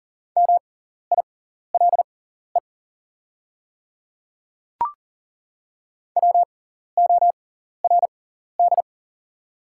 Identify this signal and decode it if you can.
Morse code, heard as a keyed tone, about 40 wpm, spaced as for 14 wpm, 690 Hz. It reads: MILE WORD
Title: 6x Character Spacing - Top 500 Words - 40wpm